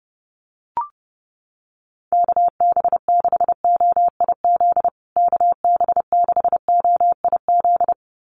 Morse code, heard as a keyed tone, about 30 wpm, 700 Hz. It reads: KB6OIZ KB6OIZ
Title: USA Call Signs Repeated 2x 30wpm